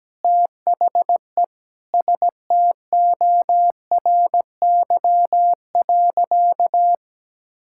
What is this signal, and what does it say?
Morse code, heard as a keyed tone, about 17 wpm, 700 Hz. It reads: THE STORY.
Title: Sentences from Top 1000 Words - Review Part 5 - 17wpm